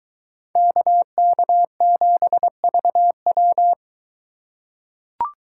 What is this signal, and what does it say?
Morse code, heard as a keyed tone, about 23 wpm, 700 Hz. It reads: KK7VW